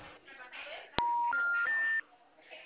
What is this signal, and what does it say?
Special information tone, with speech.